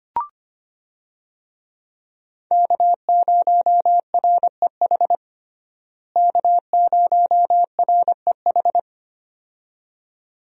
Morse code, heard as a keyed tone, about 25 wpm, 700 Hz. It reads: K0REH K0REH
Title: USA Call Signs - 5 Characters - Repeated 2x - 25wpm